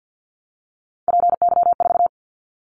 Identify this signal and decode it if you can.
Morse code, heard as a keyed tone, about 50 wpm, 700 Hz. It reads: PY4
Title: Callsign Prefixes for Top 100 DXCC - 50wpm